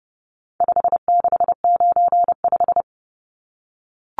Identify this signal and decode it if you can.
Morse code, heard as a keyed tone, about 29 wpm, 700 Hz. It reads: 5695
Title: Numbers - 4-Digit 30wpm